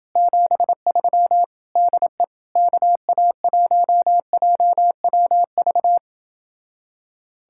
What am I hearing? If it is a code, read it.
Morse code, heard as a keyed tone, about 28 wpm, 700 Hz. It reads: 73 DE KA1JWV